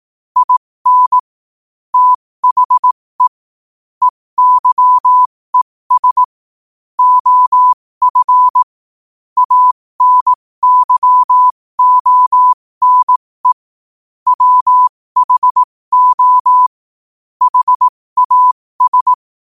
Morse code, heard as a keyed tone, about 18 wpm, spaced as for 15 wpm, 1000 Hz. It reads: IN THE EYES OF ANYONE WHO HAS
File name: qotdfiles / 15wpm